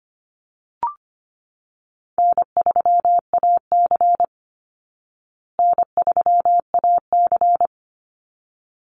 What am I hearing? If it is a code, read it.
Morse code, heard as a keyed tone, about 25 wpm, 700 Hz. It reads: N3AC N3AC
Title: Easy Call Signs Repeated 2x - 25wpm